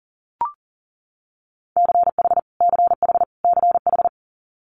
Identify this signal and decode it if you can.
Morse code, heard as a keyed tone, about 37 wpm, 700 Hz. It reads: CH CH CH
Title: Multi-letter Phonemes Repeated 3x 40wpm